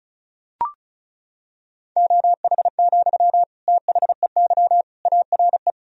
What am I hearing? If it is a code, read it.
Morse code, heard as a keyed tone, about 36 wpm, 700 Hz. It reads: OH, THEY ARE
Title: Sentences from Top 1000 Words - Review Part 4 - 35wpm